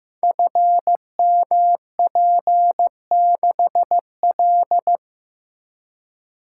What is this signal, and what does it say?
Morse code, heard as a keyed tone, about 15 wpm, 700 Hz. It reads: FMP6L
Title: Code Groups - 5 Letter:Numbers - 15wpm